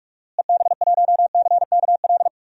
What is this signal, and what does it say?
Morse code, heard as a keyed tone, about 41 wpm, 700 Hz. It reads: EB1CKL
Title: DXCC-100 Call Signs - 6 Characters - Repeated 2x - 45wpm